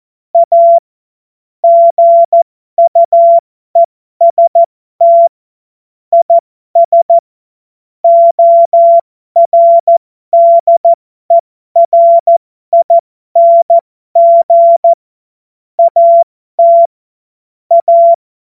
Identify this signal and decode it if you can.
Morse code, heard as a keyed tone, about 14 wpm, spaced as for 12 wpm, 680 Hz. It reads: A GUEST IS ORDERING AT A
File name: Morsekurs / Witze